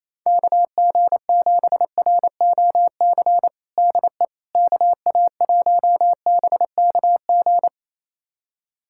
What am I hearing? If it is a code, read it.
Morse code, heard as a keyed tone, about 27 wpm, 700 Hz. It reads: KG7ROC DE KA1BKG
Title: USA Callsign DE Callsign - 6 Characters - 28wpm